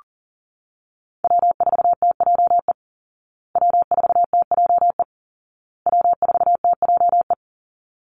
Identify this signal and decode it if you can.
Morse code, heard as a keyed tone, about 40 wpm, 700 Hz. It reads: W4TJE W4TJE W4TJE
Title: Call Signs Repeated 2x - 40wpm